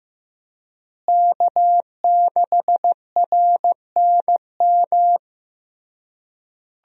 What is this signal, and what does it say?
Morse code, heard as a keyed tone, about 15 wpm, 700 Hz. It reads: K6RNM
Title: USA Call Signs - 5 Characters - Repeated 3x - 15wpm